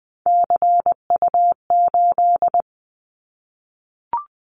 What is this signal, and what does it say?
Morse code, heard as a keyed tone, about 20 wpm, 700 Hz. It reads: CU8
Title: Callsign Prefixes for Top 50 DXCC - 20wpm